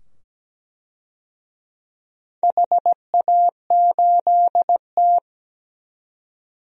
Morse code, heard as a keyed tone, about 17 wpm, 700 Hz. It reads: HA8T
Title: DXCC-100 Call Signs - 4 Characters - 17wpm